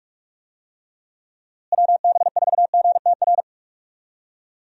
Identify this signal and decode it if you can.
Morse code, heard as a keyed tone, about 46 wpm, 700 Hz. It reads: WB4GTR
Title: USA Call Signs - 6 Characters - Repeated 3x - 45wpm